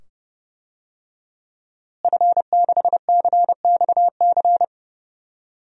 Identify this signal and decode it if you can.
Morse code, heard as a keyed tone, about 30 wpm, 700 Hz. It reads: F6CXC
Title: DXCC-100 Call Signs - 5 Characters - Repeated 2x - 30wpm